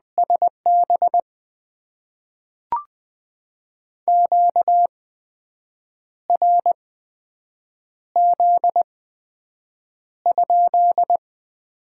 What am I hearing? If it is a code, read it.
Morse code, heard as a keyed tone, about 20 wpm, spaced as for 7 wpm, 700 Hz. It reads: SB QRZ?
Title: Top 90 QSO Elements - 8x Character Spacing - 20wpm